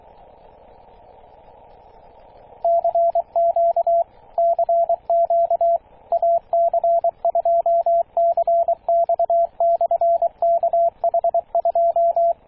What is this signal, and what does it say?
Morse code, heard as a keyed tone, about 23 wpm, 680 Hz, with radio noise. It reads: CQ CQ AC2CX/KH2
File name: posts / 2016-june-26-0520z-ac2cx-kh2-21-027mhz-cw